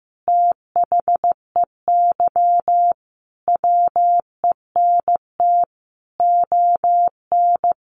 Morse code, heard as a keyed tone, about 15 wpm, 700 Hz. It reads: THEY WENT ON
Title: Sentences from Top 1400 Words 15wpm